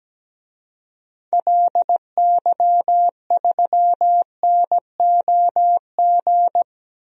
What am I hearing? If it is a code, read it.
Morse code, heard as a keyed tone, about 17 wpm, 700 Hz. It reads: LY3NOG
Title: DXCC-100 Call Signs - 6 Characters - 17wpm